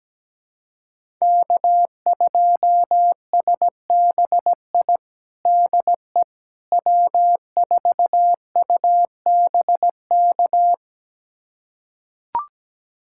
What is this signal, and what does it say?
Morse code, heard as a keyed tone, about 17 wpm, 700 Hz. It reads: K2SBI DE W4UBK
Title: USA Callsign DE Callsign - 5 Characters - 17wpm